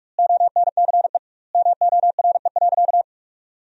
Morse code, heard as a keyed tone, about 41 wpm, 700 Hz. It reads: ONCE MORE.
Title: 2x Word-Spacing - Sentences from Top 500 Words - 45wpm